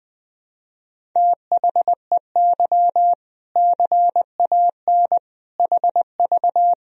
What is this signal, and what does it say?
Morse code, heard as a keyed tone, about 20 wpm, 700 Hz. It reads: THEY CAN HV
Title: Sentences from Top 400 Words and Abbreviations 20wpm